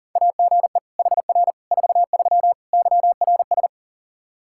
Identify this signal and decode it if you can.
Morse code, heard as a keyed tone, about 40 wpm, 700 Hz. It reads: AGE HR 43 YRS
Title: Practice QSO - Exchange 3 - More Info - 40wpm